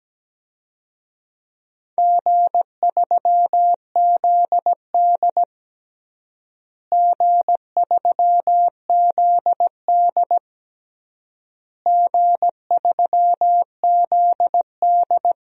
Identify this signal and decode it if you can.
Morse code, heard as a keyed tone, about 17 wpm, 700 Hz. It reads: G3ZD G3ZD G3ZD
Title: DXCC-100 Call Signs - 4 Characters - Repeated 3x - 17wpm